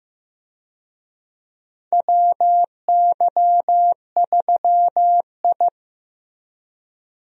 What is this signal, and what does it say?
Morse code, heard as a keyed tone, about 15 wpm, 700 Hz. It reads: WY3I
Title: USA Call Signs - 4 Characters - Repeated 3x - 15wpm